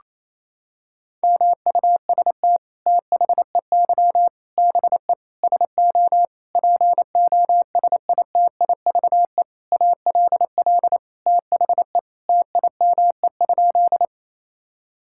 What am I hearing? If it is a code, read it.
Morse code, heard as a keyed tone, about 28 wpm, 700 Hz. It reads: MUST THEY BE SO POSITIVE ALL THE TIME?
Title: Sentences from Top 1000 Words - Review Part 5 - 28wpm